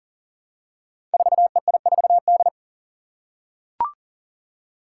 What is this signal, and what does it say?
Morse code, heard as a keyed tone, about 42 wpm, 700 Hz. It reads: 4EI4D
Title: Five Challenging Letter-Number Combinations 40wpm